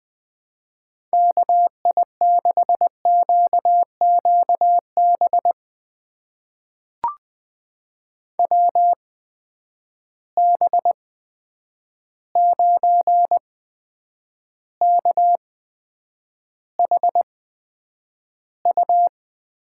Morse code, heard as a keyed tone, about 20 wpm, spaced as for 9 wpm, 700 Hz. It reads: KI6QQB WB9KHU
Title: USA Call Signs - 6 Characters - 8x Character Spacing - 20wpm